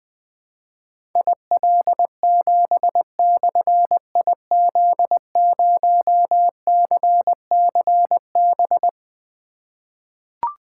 Morse code, heard as a keyed tone, about 20 wpm, 700 Hz. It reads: IL7/IZ0CCB
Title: Difficult Call Signs 20wpm